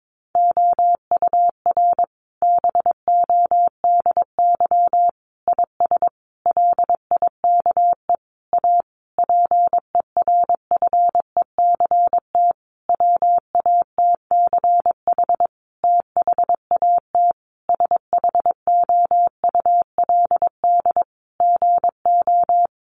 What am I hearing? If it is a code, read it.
Morse code, heard as a keyed tone, about 22 wpm, 700 Hz. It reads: OUR BODY IS LIKE A PERFECT WATCH THAT SHOULD GO
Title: Sentences from Top 1000 Words 22wpm